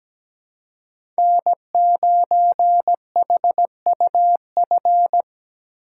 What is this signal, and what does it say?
Morse code, heard as a keyed tone, about 17 wpm, 700 Hz. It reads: N9HUF
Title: USA Call Signs - 5 Characters - 4x Character Spacing - 17wpm